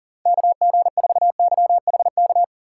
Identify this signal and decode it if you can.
Morse code, heard as a keyed tone, about 40 wpm, 700 Hz. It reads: KG4YHK
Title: USA Call Signs - 6 Characters - 3x Character Spacing - 40wpm